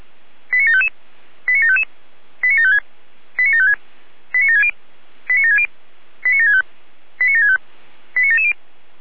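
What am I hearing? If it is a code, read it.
Five-tone selective call (ZVEI), heard as quick runs of five tones, tones of about 70 ms. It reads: 89755 89755 89756 89756 89760 89760 89765 89765 89770
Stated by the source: Five-tone ZVEI selective call received on an SDR.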